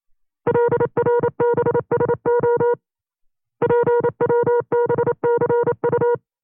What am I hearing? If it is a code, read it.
Morse code, heard as a keyed tone, about 28 wpm, 480 Hz. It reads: LRBSO PWBCU
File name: CW-practice/25WPM